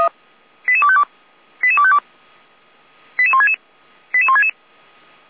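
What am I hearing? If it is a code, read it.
Five-tone selective call (ZVEI), heard as quick runs of five tones, tones of about 70 ms. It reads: 88252 88252 88166 88166
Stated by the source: ZVEI five-tone call 88249 received on SDR, served by Feuerwehr Orlamünde.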